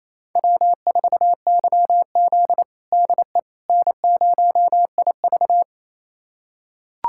Morse code, heard as a keyed tone, about 28 wpm, 700 Hz. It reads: W4YZ DE N0IV
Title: USA Callsign DE Callsign - 4 Characters - 28wpm